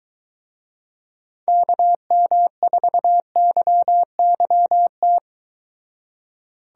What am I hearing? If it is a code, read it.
Morse code, heard as a keyed tone, about 23 wpm, 700 Hz. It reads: KM4YYT